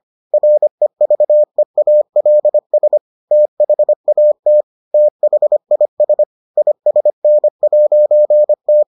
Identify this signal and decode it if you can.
Morse code, heard as a keyed tone, about 25 wpm, 600 Hz. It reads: REVEALS THAT THIS ISN'T